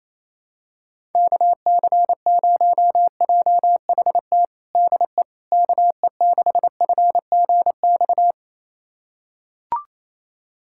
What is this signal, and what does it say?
Morse code, heard as a keyed tone, about 28 wpm, 700 Hz. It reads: KC0JHT DE KE6FGX
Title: USA Callsign DE Callsign - 6 Characters - 28wpm